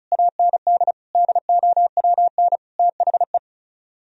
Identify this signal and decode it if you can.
Morse code, heard as a keyed tone, about 46 wpm, 700 Hz. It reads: AND DOWN THE